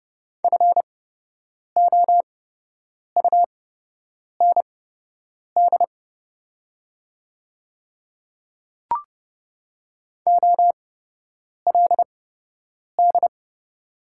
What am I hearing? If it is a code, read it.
Morse code, heard as a keyed tone, about 30 wpm, spaced as for 8 wpm, 700 Hz. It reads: FOUND OLD
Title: 8x Character Spacing - Top 300 Words - 30wpm